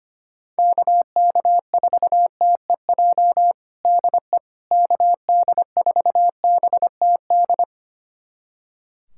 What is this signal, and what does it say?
Morse code, heard as a keyed tone, about 25 wpm, 700 Hz. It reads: KK4TEJ DE KD4BTD
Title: USA Callsign DE Callsign - 6 Characters - 25wpm